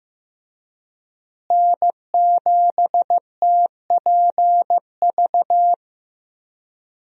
Morse code, heard as a keyed tone, about 15 wpm, 700 Hz. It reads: N7TPV
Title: USA Call Signs - 5 Characters - Repeated 2x - 15wpm